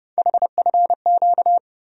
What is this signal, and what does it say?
Morse code, heard as a keyed tone, about 31 wpm, 700 Hz. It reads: HFQ
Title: USA Call Sign Postfixes - 3 Letters 30wpm